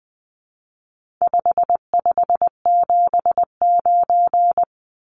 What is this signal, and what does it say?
Morse code, heard as a keyed tone, about 20 wpm, 700 Hz. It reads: 5579
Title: Numbers - 4-Digit 20wpm